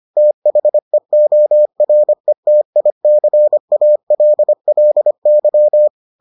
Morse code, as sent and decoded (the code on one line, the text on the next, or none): - .... --- .-. - .. -.-. .- .-.. .-.. -.--
THEORETICALLY